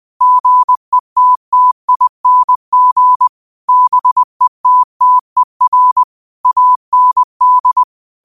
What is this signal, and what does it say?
Morse code, heard as a keyed tone, about 20 wpm, 1000 Hz. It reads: GETTING BETTER AND